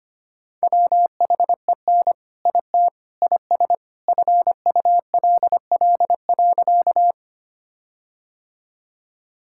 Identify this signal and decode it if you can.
Morse code, heard as a keyed tone, about 25 wpm, 700 Hz. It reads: WHEN IT IS FULL.
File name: practice / 25wpm / All-Age Appropriate Jokes - 25wpm